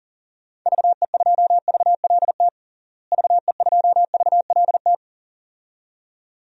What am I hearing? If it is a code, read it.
Morse code, heard as a keyed tone, about 40 wpm, 700 Hz. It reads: VE2VLT VE2VLT
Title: DXCC-100 Call Signs - 6 Characters - Repeated 2x - 40wpm